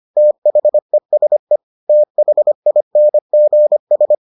Morse code, heard as a keyed tone, about 25 wpm, 600 Hz. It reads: THESE THINGS